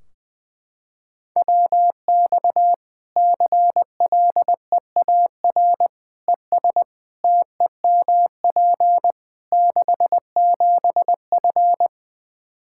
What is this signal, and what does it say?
Morse code, heard as a keyed tone, about 20 wpm, 700 Hz. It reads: WX CLEAR ES TEMP 67F
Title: K1USN SST QSO Practice 20wpm